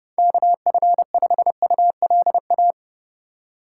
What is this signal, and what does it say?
Morse code, heard as a keyed tone, about 30 wpm, 700 Hz. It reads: KF5ULA
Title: USA Call Signs - 4x Character Spacing - 30wpm